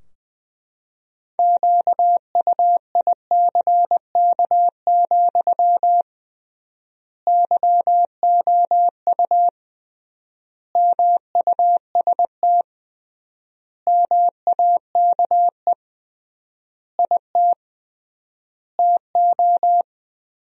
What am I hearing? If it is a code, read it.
Morse code, heard as a keyed tone, about 21 wpm, 700 Hz. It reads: QUICK, YOU MUST MAKE IT TO
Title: 3x Word-Spacing - Sentences from Top 500 Words - 20wpm